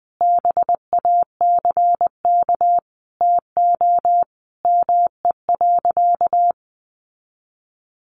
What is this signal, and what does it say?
Morse code, heard as a keyed tone, about 20 wpm, 700 Hz. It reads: BACK TO ME.
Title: Sentences from Top 600 Words 20wpm